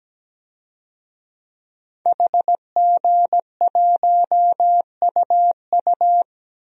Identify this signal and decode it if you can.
Morse code, heard as a keyed tone, about 17 wpm, 700 Hz. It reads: HG1UU
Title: DXCC-100 Call Signs - 5 Characters - 17wpm